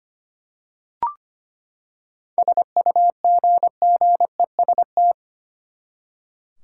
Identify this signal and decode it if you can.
Morse code, heard as a keyed tone, about 25 wpm, 700 Hz. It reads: SUGGEST
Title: Seven Letter Words - 25wpm